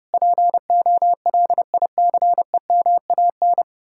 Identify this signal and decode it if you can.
Morse code, heard as a keyed tone, about 40 wpm, 700 Hz. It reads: POLICEMAN